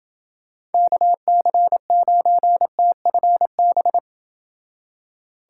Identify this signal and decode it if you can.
Morse code, heard as a keyed tone, about 29 wpm, 700 Hz. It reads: KC9TFB